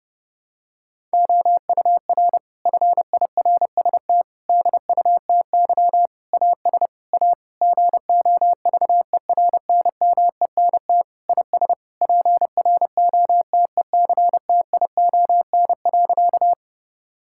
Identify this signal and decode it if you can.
Morse code, heard as a keyed tone, about 30 wpm, 700 Hz. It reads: OUR FIRST DUTY AS A GOVERNMENT IS PROTECTION.
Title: Sentences from Top 1600 Words 30wpm